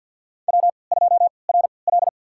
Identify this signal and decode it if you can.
Morse code, heard as a keyed tone, about 52 wpm, spaced as for 28 wpm, 700 Hz. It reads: W2RL
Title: USA Call Signs - 4 Characters - 3x Character Spacing - 50wpm